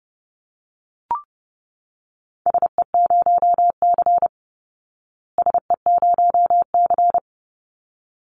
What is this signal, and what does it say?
Morse code, heard as a keyed tone, about 30 wpm, 700 Hz. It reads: SE0C SE0C
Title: Call Signs Repeated 3x - 30wpm